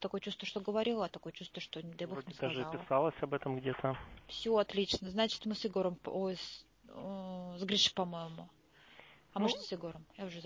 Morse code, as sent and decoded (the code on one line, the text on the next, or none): none